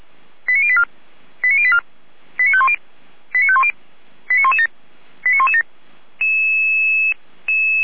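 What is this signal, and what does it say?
Five-tone selective call (ZVEI), heard as quick runs of five tones, tones of about 70 ms. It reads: 89084 89084 89510 89510 89117 89117
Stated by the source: ZVEI five-tone selective call received on an SDR, from the Feuerwehr Orlamünde audio archive.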